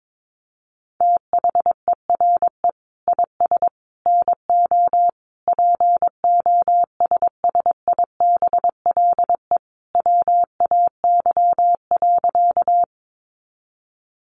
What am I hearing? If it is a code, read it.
Morse code, heard as a keyed tone, about 22 wpm, 700 Hz. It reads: THERE IS NO POSSIBLE WAY.
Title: Sentences from Top 500 Words 22wpm